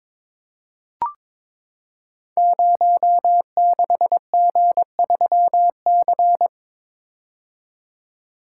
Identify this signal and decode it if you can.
Morse code, heard as a keyed tone, about 22 wpm, 700 Hz. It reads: 06G3C
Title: Code Groups - 5 Letter:Numbers - 22wpm